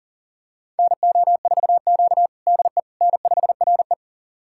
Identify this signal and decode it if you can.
Morse code, heard as a keyed tone, about 40 wpm, 700 Hz. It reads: NO4Q DE N5RE
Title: USA Callsign DE Callsign - 4 Characters - 40wpm